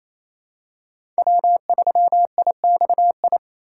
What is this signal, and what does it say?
Morse code, heard as a keyed tone, about 28 wpm, 700 Hz. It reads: W3IXI
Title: USA Call Signs - 5 Characters - 2x Character Spacing - 28wpm